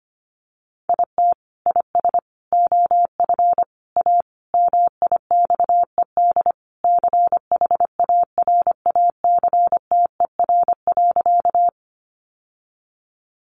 Morse code, heard as a keyed tone, about 25 wpm, 700 Hz. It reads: IT IS OF A MIXED CHARACTER.
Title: Sentences from Top 1500 Words 25wpm